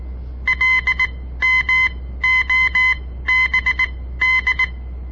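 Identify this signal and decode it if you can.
Morse code, heard as a keyed tone, about 18 wpm, spaced as for 14 wpm, 2050 Hz. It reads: LMOBD
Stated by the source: On-air Morse code practice broadcast by a southern Montana amateur radio association; all code, no read back.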